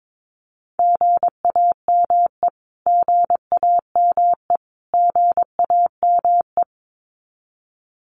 Morse code, heard as a keyed tone, about 22 wpm, 700 Hz. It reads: GAME GAME GAME
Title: Top 500 Words Repeated 22wpm